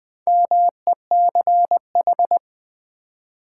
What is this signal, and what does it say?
Morse code, heard as a keyed tone, about 20 wpm, 700 Hz. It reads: MECH
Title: Build A Word - 4 Letters At A Time - 10-Letter Words - 20wpm